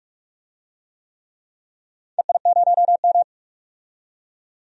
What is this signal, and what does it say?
Morse code, heard as a keyed tone, about 46 wpm, 700 Hz. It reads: EI0M
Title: DXCC-100 Call Signs - 4 Characters - 45wpm